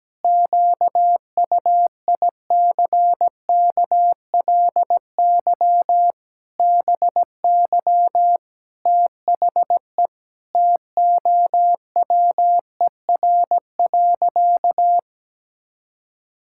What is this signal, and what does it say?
Morse code, heard as a keyed tone, about 17 wpm, 700 Hz. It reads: QUICKLY BY THE TOWER.
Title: Pangram Challenge v2 - 17wpm